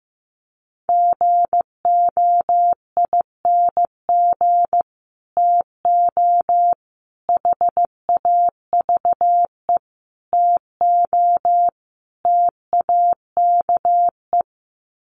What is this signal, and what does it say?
Morse code, heard as a keyed tone, about 15 wpm, 700 Hz. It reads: GOING TO HAVE TO TAKE
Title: Sets of 5 Words Encore 15wpm